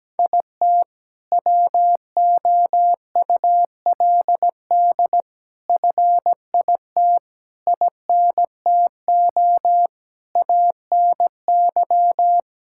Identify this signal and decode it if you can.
Morse code, heard as a keyed tone, about 17 wpm, 700 Hz. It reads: IT WOULD FIT INTO ANY
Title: Sentences from Top 1000 Words - Review Part 5 - 17wpm